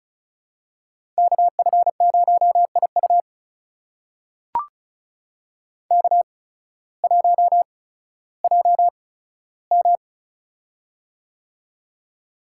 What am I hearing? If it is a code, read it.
Morse code, heard as a keyed tone, about 36 wpm, spaced as for 16 wpm, 700 Hz. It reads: KF0IU K1JM
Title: USA Call Signs - 8x Character Spacing - 35wpm